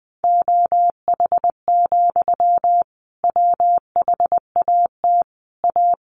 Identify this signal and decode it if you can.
Morse code, heard as a keyed tone, about 19 wpm, 700 Hz. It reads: OH, WHAT A
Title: Sentences from Top 1100 Words 20wpm